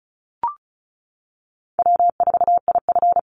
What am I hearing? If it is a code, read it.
Morse code, heard as a keyed tone, about 34 wpm, 700 Hz. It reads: W4IF
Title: Call Signs - 35wpm